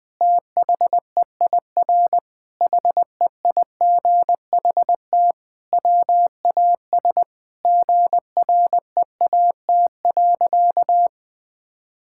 Morse code, heard as a keyed tone, about 20 wpm, 700 Hz. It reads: THEIR HEIGHT WAS GREAT.
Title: Sentences from Top 1600 Words 20wpm